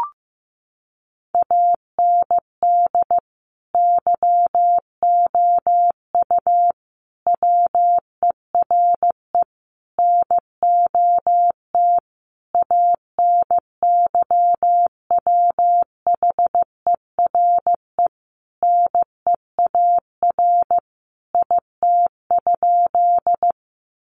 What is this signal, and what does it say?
Morse code, heard as a keyed tone, about 14 wpm, 700 Hz. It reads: AND YOU WERE NOT ANYWHERE NEAR IT?
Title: Sentences from Top 1400 Words 15wpm